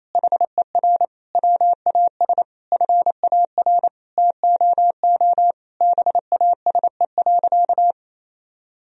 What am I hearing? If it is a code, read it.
Morse code, heard as a keyed tone, about 28 wpm, 700 Hz. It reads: HER WAS FAR TOO BASE.
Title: Sentences from Top 1000 Words - Review Part 1 - 28wpm